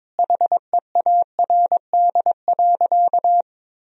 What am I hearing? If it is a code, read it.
Morse code, heard as a keyed tone, about 22 wpm, 700 Hz. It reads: HEARD.
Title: Sentences from Top 400 Words and Abbreviations 22wpm